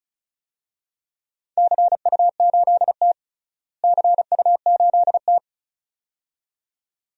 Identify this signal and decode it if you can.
Morse code, heard as a keyed tone, about 36 wpm, 700 Hz. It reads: CU8T CU8T
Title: DXCC-100 Call Signs - 4 Characters - Repeated 2x - 35wpm